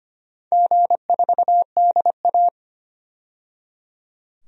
Morse code, heard as a keyed tone, about 25 wpm, 700 Hz. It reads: G4DA
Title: DXCC-100 Call Signs - 4 Characters - 25wpm